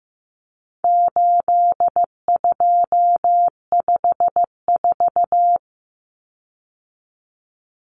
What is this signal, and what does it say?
Morse code, heard as a keyed tone, about 15 wpm, 700 Hz. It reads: 8254
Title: Common Cut Numbers - 4-Digit 15wpm